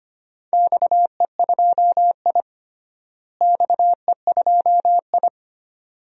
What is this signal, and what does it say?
Morse code, heard as a keyed tone, about 25 wpm, 700 Hz. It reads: XE2I XE2I
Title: DXCC-100 Call Signs - 4 Characters - Repeated 2x - 25wpm